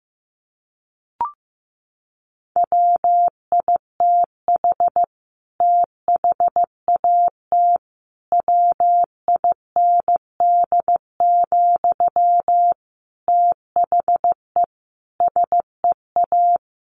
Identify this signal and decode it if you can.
Morse code, heard as a keyed tone, about 15 wpm, 700 Hz. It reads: WITH THAT WIND, THE SEA
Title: Sentences from Top 1400 Words 15wpm